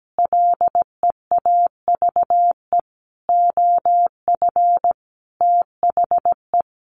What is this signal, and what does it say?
Morse code, heard as a keyed tone, about 17 wpm, 700 Hz. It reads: LEAVE OF THE
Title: Sentences from Top 1400 Words 17wpm